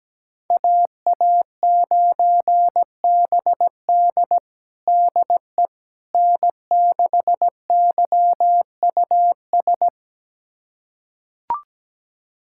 Morse code, heard as a keyed tone, about 17 wpm, 700 Hz. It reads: AA9BD DE N6YUS
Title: USA Callsign DE Callsign 17wpm